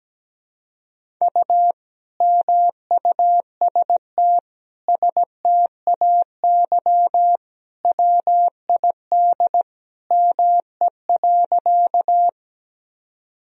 Morse code, heard as a keyed tone, about 17 wpm, 700 Hz. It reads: U MUST STAY WID ME.
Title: Sentences from Top 500 Words and Abbreviations 17wpm